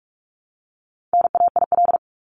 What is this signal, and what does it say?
Morse code, heard as a keyed tone, about 44 wpm, 700 Hz. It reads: NAIL
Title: Top 1501-1750 Words 45wpm